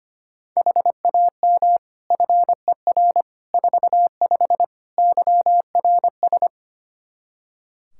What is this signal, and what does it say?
Morse code, heard as a keyed tone, about 25 wpm, 700 Hz. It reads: HAM FER 45 YRS
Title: K1USN SST QSO Practice 25wpm